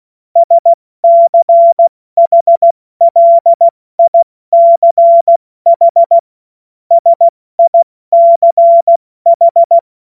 Morse code, heard as a keyed tone, about 16 wpm, 680 Hz. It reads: SCHLICH SICH
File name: Morsekurs / Witze / Witz 07. März 2023 70 F80